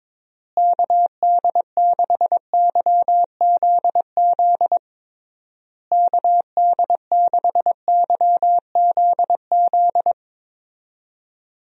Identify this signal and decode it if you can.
Morse code, heard as a keyed tone, about 22 wpm, 700 Hz. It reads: KD6YZZ KD6YZZ
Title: USA Call Signs - 6 Characters - Repeated 3x - 22wpm